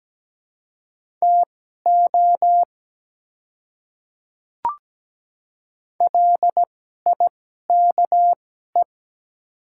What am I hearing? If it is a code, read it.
Morse code, heard as a keyed tone, about 17 wpm, spaced as for 12 wpm, 700 Hz. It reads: TO LIKE
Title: 2x Character Spacing - Top 200 Words - 17wpm